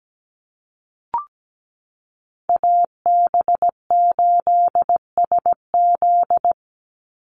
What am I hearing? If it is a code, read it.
Morse code, heard as a keyed tone, about 17 wpm, 700 Hz. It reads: AB8SZ
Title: USA Call Signs 17wpm